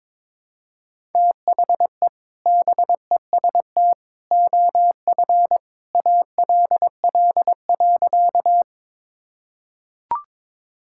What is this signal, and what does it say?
Morse code, heard as a keyed tone, about 22 wpm, 700 Hz. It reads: THE BEST OF ALL.
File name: practice / 22wpm / Sentences with 2-Word Phrases - 22wpm